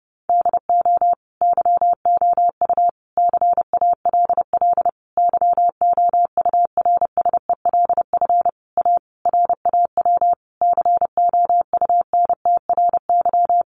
Morse code, heard as a keyed tone, about 28 wpm, 700 Hz. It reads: DO YOU CALL YOURSELF A RAW COUNTRY
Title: Sentences from Top 1500 Words 30wpm